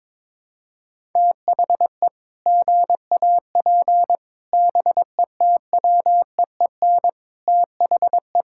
Morse code, heard as a keyed tone, about 22 wpm, 700 Hz. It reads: THE GAP BETWEEN THE
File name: practice / 22wpm / Sets of 8 Words - 22wpm